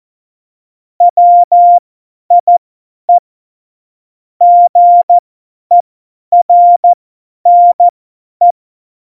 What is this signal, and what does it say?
Morse code, heard as a keyed tone, about 14 wpm, spaced as for 10 wpm, 700 Hz. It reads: WIE GERNE
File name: Morsekurs / Witze / Witz 22. Mai 2018, 50 F70